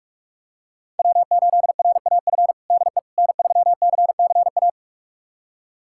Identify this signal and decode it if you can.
Morse code, heard as a keyed tone, about 45 wpm, 700 Hz. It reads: W8RAF DE N3CCA
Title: USA Callsign DE Callsign - 5 Characters - 45wpm